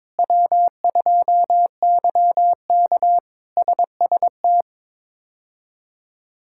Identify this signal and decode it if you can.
Morse code, heard as a keyed tone, about 23 wpm, 700 Hz. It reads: W2YK SST
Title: K1USN SST QSO Practice - 22wpm